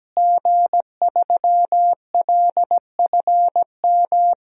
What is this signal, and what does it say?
Morse code, heard as a keyed tone, about 17 wpm, 700 Hz. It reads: G3LFM
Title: DXCC-100 Call Signs - 5 Characters - Repeated 3x - 17wpm